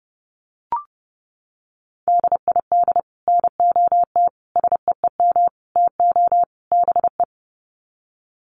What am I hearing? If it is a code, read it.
Morse code, heard as a keyed tone, about 30 wpm, 700 Hz. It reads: DID NOT SEEM TO BE
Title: Sets of 5 Words Encore 30wpm